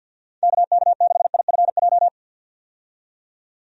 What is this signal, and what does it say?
Morse code, heard as a keyed tone, about 51 wpm, 700 Hz. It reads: KK6IFJ
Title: USA Call Signs - 6 Characters - 50wpm